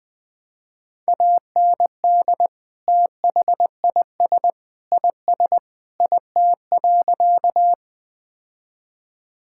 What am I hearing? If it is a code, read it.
Morse code, heard as a keyed tone, about 20 wpm, 700 Hz. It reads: AND THIS IS IT.
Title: Sentences with 3-Word Phrases - Spoken Beginning - 20wpm